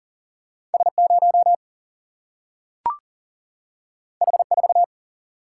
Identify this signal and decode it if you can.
Morse code, heard as a keyed tone, about 40 wpm, 700 Hz. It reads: S0 H4
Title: Single-Letter Single-Number 40wpm